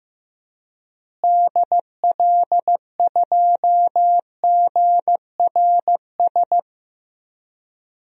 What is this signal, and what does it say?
Morse code, heard as a keyed tone, about 15 wpm, 700 Hz. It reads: DL2GRS
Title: DXCC-100 Call Signs - 6 Characters - Repeated 3x - 15wpm